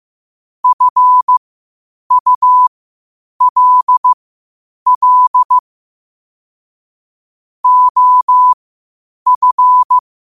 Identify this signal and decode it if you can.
Morse code, heard as a keyed tone, about 15 wpm, spaced as for 8 wpm, 1000 Hz. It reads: FULL OF
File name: qotdfiles / 8wpm